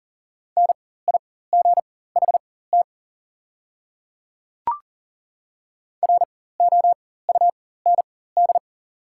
Morse code, heard as a keyed tone, about 40 wpm, spaced as for 19 wpm, 700 Hz. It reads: NIGHT ROUND
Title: 4x Character Spacing - Top 300 Words - 40wpm